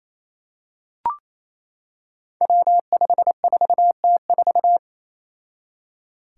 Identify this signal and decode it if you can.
Morse code, heard as a keyed tone, about 28 wpm, 700 Hz. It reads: W54T4
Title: Five Challenging Letter-Number Combinations 28wpm